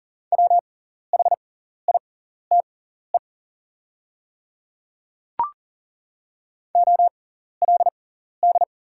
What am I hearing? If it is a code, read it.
Morse code, heard as a keyed tone, about 40 wpm, spaced as for 14 wpm, 690 Hz. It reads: WHITE OLD
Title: 6x Character Spacing - Top 300 Words - 40wpm